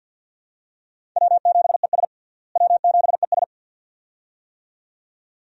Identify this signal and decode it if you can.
Morse code, heard as a keyed tone, about 47 wpm, 700 Hz. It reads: W7ES W7ES
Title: USA Call Signs - 4 Characters - Repeated 2x - 50wpm